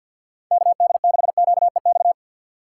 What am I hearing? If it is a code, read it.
Morse code, heard as a keyed tone, about 51 wpm, 700 Hz. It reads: KD6QEX
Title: USA Call Signs - 3x Character Spacing - 50wpm